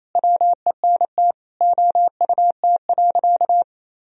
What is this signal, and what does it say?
Morse code, heard as a keyed tone, about 28 wpm, 700 Hz. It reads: WENT OUT.
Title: Sentences from Top 1000 Words - Review Part 1 - 28wpm